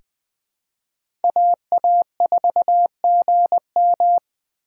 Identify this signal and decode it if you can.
Morse code, heard as a keyed tone, about 20 wpm, 700 Hz. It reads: AA4GM
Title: USA Call Signs - 3x Character Spacing - 20wpm